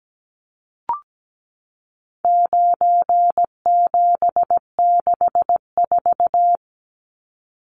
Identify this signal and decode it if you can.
Morse code, heard as a keyed tone, about 17 wpm, 700 Hz. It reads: 9764
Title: Numbers - 4-Digit 17wpm